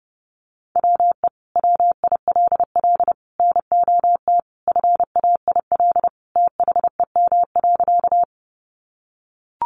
Morse code, heard as a keyed tone, about 30 wpm, 700 Hz. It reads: WE WILL NOT FAIL THEM.
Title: Sentences from Top 1100 Words 30wpm